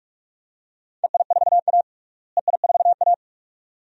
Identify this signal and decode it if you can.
Morse code, heard as a keyed tone, about 41 wpm, 700 Hz. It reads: EI4A EI4A
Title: DXCC-100 Call Signs - 4 Characters - Repeated 2x - 45wpm